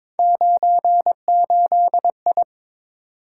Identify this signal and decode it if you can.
Morse code, heard as a keyed tone, about 23 wpm, 700 Hz. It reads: 98I
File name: practice / 22wpm / Three Letter-Number Combinations - 22wpm